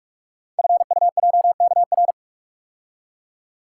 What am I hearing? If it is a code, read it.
Morse code, heard as a keyed tone, about 44 wpm, 690 Hz. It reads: FUJKR
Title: Code Groups - 5 Letters - 45wpm